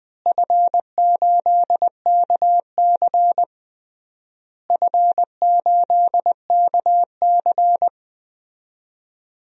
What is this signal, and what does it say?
Morse code, heard as a keyed tone, about 20 wpm, 700 Hz. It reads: F8KC F8KC
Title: DXCC-100 Call Signs - 4 Characters - Repeated 3x - 20wpm